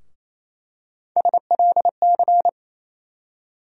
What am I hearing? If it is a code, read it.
Morse code, heard as a keyed tone, about 28 wpm, 700 Hz. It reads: SLC
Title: USA Call Sign Postfixes - 3 Letters 28wpm